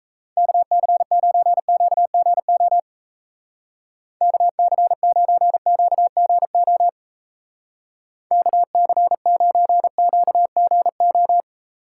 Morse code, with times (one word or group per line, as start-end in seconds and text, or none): 0.4-2.8 KC9QGO
4.2-6.9 KC9QGO
8.3-11.4 KC9QGO